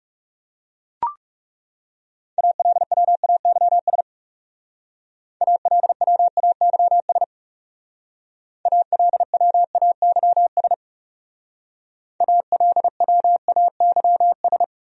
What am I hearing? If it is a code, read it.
Morse code, heard as a keyed tone, about 35 wpm, 700 Hz. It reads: ALWAYS ALWAYS ALWAYS ALWAYS
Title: Speed-Racing Top 300 Words 30wpm